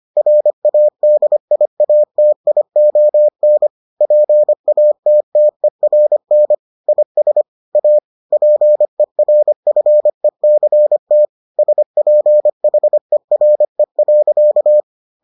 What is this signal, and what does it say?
Morse code, heard as a keyed tone, about 25 wpm, 600 Hz. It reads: RADIATION PATTERN IS A PERFECT SPHERE.